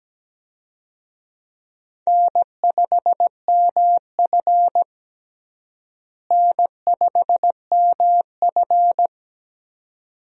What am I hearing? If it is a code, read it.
Morse code, heard as a keyed tone, about 17 wpm, 700 Hz. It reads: N5MF N5MF
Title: USA Call Signs - 4 Characters - Repeated 2x - 17wpm